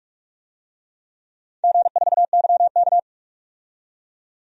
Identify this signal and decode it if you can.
Morse code, heard as a keyed tone, about 45 wpm, 700 Hz. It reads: G4YK
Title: DXCC-100 Call Signs - 4 Characters - 45wpm